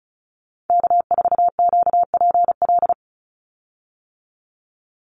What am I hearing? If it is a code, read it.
Morse code, heard as a keyed tone, about 36 wpm, 700 Hz. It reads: K4QPL
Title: Call Signs - 35wpm